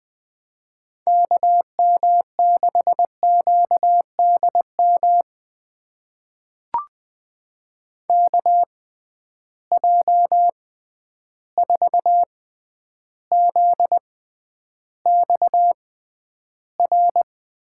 Morse code, heard as a keyed tone, about 21 wpm, spaced as for 10 wpm, 700 Hz. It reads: KM6QDM KJ4ZXR
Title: USA Call Signs - 6 Characters - 6x Character Spacing - 20wpm